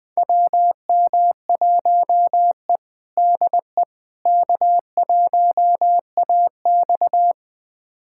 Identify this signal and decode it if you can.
Morse code, heard as a keyed tone, about 20 wpm, 700 Hz. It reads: WM1E DE K1AX
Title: USA Callsign DE Callsign - 4 Characters - 20wpm